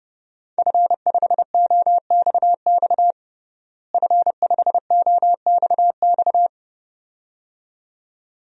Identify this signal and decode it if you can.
Morse code, heard as a keyed tone, about 32 wpm, 700 Hz. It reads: F5OXX F5OXX
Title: DXCC-100 Call Signs - 5 Characters - Repeated 2x - 30wpm